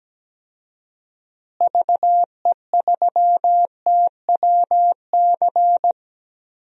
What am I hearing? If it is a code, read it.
Morse code, heard as a keyed tone, about 17 wpm, 700 Hz. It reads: VE3TWC
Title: DXCC-100 Call Signs - 6 Characters - Repeated 3x - 17wpm